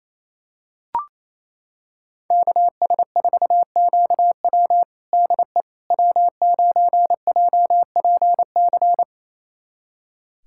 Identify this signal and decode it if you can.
Morse code, heard as a keyed tone, about 28 wpm, 700 Hz. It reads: KS4QW DE W9JPC
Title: USA Callsign DE Callsign - 5 Characters - 28wpm